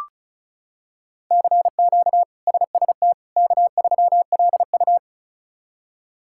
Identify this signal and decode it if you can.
Morse code, heard as a keyed tone, about 34 wpm, 700 Hz. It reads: CQ SST K3LU
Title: K1USN SST QSO Practice - 35wpm